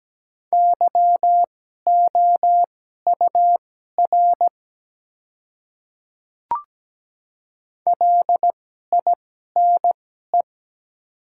Morse code, heard as a keyed tone, about 17 wpm, spaced as for 12 wpm, 700 Hz. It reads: YOUR LINE
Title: 2x Character Spacing - Top 200 Words - 17wpm